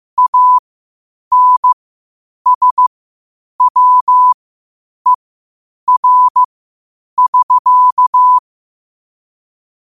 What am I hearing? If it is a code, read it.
Morse code, heard as a keyed tone, about 15 wpm, spaced as for 8 wpm, 1000 Hz. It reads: ANSWER <SK>
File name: qotdfiles / 8wpm